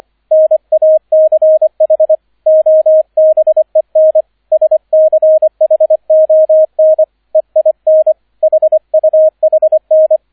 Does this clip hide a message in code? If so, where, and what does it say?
Yes: Morse code from 0.3 to 10.2 s, heard as a keyed tone, about 25 wpm, 630 Hz, with radio noise. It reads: NACH OBEN SCHON EIN HUHN